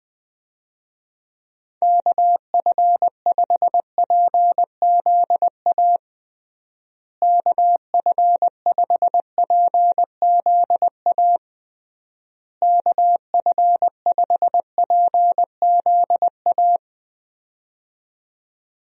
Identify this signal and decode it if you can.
Morse code, heard as a keyed tone, about 21 wpm, 700 Hz. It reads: KF5PZA KF5PZA KF5PZA
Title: USA Call Signs - 6 Characters - Repeated 3x - 20wpm